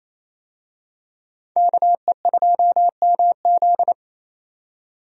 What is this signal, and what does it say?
Morse code, heard as a keyed tone, about 28 wpm, 700 Hz. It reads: KE2MZ
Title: USA Call Signs - 5 Characters - 28wpm